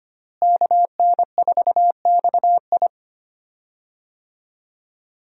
Morse code, heard as a keyed tone, about 25 wpm, 700 Hz. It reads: KN4XI
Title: USA Call Signs - 5 Characters - Repeated 3x - 25wpm